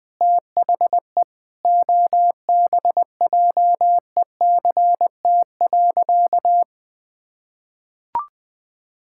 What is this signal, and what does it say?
Morse code, heard as a keyed tone, about 20 wpm, 700 Hz. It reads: THE OBJECT.
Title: Sentences from Top 1000 Words - Review Part 5 - 20wpm